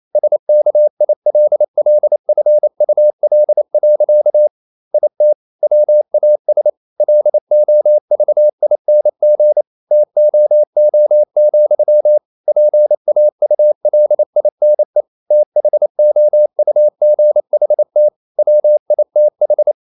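Morse code, heard as a keyed tone, about 28 wpm, 600 Hz. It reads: SKILLFUL. IT WAS LOVING TOO, PAULINE THOUGHT WITH
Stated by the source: Morse code practice at 28 wpm, text of Ken Follett's novel Never in English, from CW med Gnister.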